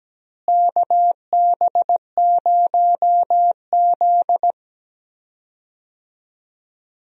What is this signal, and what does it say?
Morse code, heard as a keyed tone, about 17 wpm, 700 Hz. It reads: KB0Z
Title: USA Call Signs - 4 Characters - Repeated 2x - 17wpm